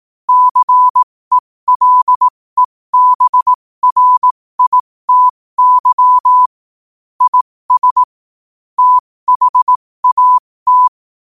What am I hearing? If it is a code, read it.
Morse code, heard as a keyed tone, about 18 wpm, spaced as for 15 wpm, 1000 Hz. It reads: CELEBRITY IS THAT
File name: qotdfiles / 15wpm